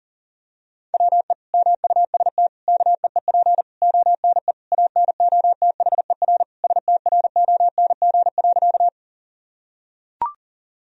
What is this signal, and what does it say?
Morse code, heard as a keyed tone, about 40 wpm, 700 Hz. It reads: WE MUST KEEP ONE ANOTHER STRONG.
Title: Sentences from Top 1000 Words - Review Part 3 - 40wpm